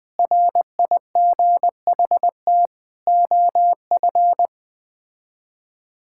Morse code, heard as a keyed tone, about 21 wpm, 700 Hz. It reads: RIGHT OF
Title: Sets of 8 Words - 20wpm